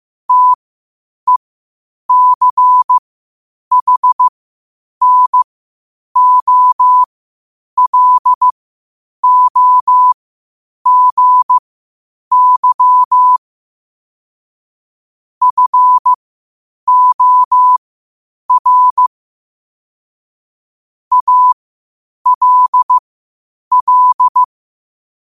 Morse code, heard as a keyed tone, about 15 wpm, spaced as for 8 wpm, 1000 Hz. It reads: TECHNOLOGY FOR ALL
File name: qotdfiles / 8wpm